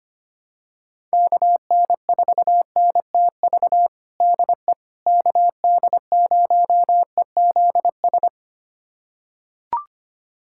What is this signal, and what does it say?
Morse code, heard as a keyed tone, about 25 wpm, 700 Hz. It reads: KN4NTV DE KD0EZS
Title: USA Callsign DE Callsign 25wpm